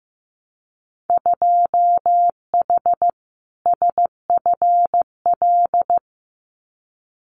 Morse code, heard as a keyed tone, about 15 wpm, 700 Hz. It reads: 2H SFL
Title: Winter Field Day Exchanges Encore 15wpm